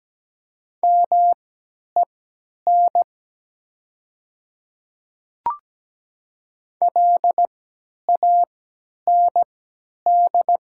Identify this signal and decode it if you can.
Morse code, heard as a keyed tone, about 17 wpm, spaced as for 10 wpm, 700 Hz. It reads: MEN LAND
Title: 3x Character Spacing - Top 200 Words - 17wpm